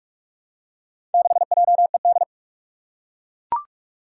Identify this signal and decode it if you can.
Morse code, heard as a keyed tone, about 41 wpm, 700 Hz. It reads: 6JED
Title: Four Challenging Letter-Number Combinations 45wpm